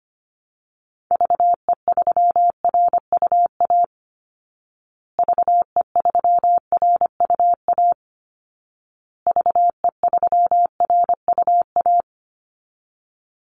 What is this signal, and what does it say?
Morse code, heard as a keyed tone, about 25 wpm, 700 Hz. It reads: VE3RUA VE3RUA VE3RUA
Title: Call Signs Repeated 2x - 25wpm